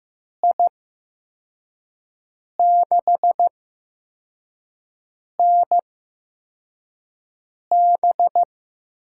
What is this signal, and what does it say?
Morse code, heard as a keyed tone, about 15 wpm, spaced as for 4 wpm, 700 Hz. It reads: I6NB
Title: USA Call Signs - 5 Characters - 8x Character Spacing - 15wpm